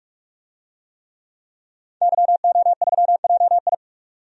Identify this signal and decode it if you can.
Morse code, heard as a keyed tone, about 46 wpm, 700 Hz. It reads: YO3JI
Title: DXCC-100 Call Signs - 5 Characters - 45wpm